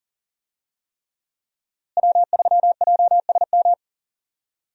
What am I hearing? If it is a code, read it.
Morse code, heard as a keyed tone, about 40 wpm, 700 Hz. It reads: W3JSM
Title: USA Call Signs - 5 Characters - Repeated 3x - 40wpm